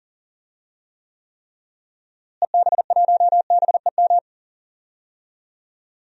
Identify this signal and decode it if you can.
Morse code, heard as a keyed tone, about 40 wpm, 700 Hz. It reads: EB1BEM
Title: DXCC-100 Call Signs - 6 Characters - Repeated 3x - 40wpm